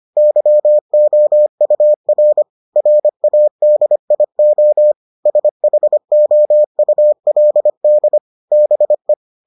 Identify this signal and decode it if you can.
Morse code, heard as a keyed tone, about 25 wpm, 600 Hz. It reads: YOUR RADIO SHOULD BE